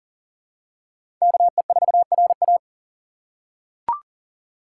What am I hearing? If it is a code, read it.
Morse code, heard as a keyed tone, about 40 wpm, 690 Hz. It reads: KE4RA
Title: USA Call Signs - 5 Characters - 6x Character Spacing - 40wpm